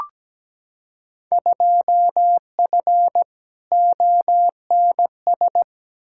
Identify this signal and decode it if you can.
Morse code, heard as a keyed tone, about 17 wpm, 700 Hz. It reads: 2F ONS
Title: Field Day Exchanges - 17wpm